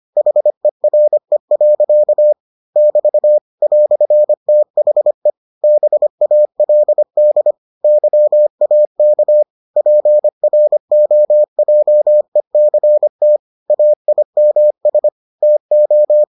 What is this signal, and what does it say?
Morse code, heard as a keyed tone, about 25 wpm, 600 Hz. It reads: HERE. <BT> "THE BALD YAK PROJECT AIMS TO